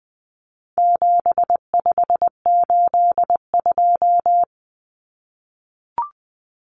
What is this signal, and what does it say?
Morse code, heard as a keyed tone, about 20 wpm, 700 Hz. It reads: 7582
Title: Numbers - 4-Digit 20wpm